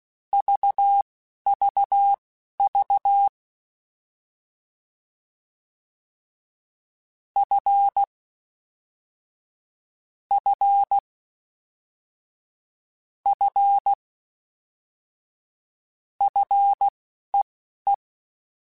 Morse code, heard as a keyed tone, about 16 wpm, spaced as for 11 wpm, 800 Hz. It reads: VVV F F F FEE
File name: Combinations / Combinations-mp3s / Combinations@10